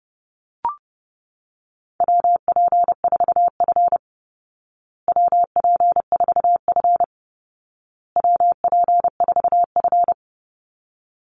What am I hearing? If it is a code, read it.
Morse code, heard as a keyed tone, about 30 wpm, 700 Hz. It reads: WP4F WP4F WP4F
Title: Call Signs Repeated 2x - 30wpm